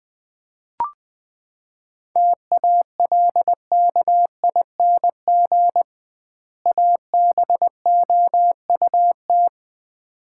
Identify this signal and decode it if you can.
Morse code, heard as a keyed tone, about 20 wpm, 700 Hz. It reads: TALKING ABOUT
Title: 2x Word-Spacing - Sets of 2 Words - 20wpm